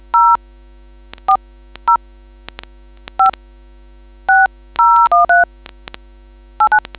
DTMF, heard as telephone touch tones, tones of about 140 ms. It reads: 040 56013 89